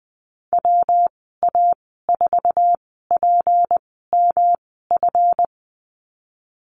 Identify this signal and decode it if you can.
Morse code, heard as a keyed tone, about 20 wpm, spaced as for 15 wpm, 700 Hz. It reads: WA4PMF
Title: 167 Lesson 27g - Number 7 - Call Signs 20wpm